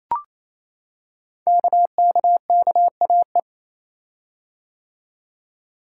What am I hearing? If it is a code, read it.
Morse code, heard as a keyed tone, about 28 wpm, 700 Hz. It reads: KKKAE
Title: Code Groups - 5 Letters - 28wpm